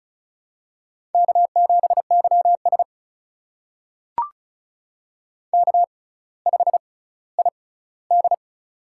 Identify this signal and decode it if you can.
Morse code, heard as a keyed tone, about 36 wpm, spaced as for 18 wpm, 700 Hz. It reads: K7YS K5ID
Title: USA Call Signs - 4 Characters - 6x Character Spacing - 35wpm